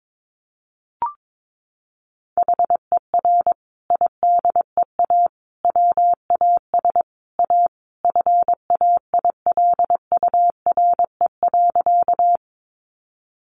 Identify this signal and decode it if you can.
Morse code, heard as a keyed tone, about 22 wpm, 700 Hz. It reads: HER IDEA WAS A FAILURE.
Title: Sentences from Top 900 Words 22wpm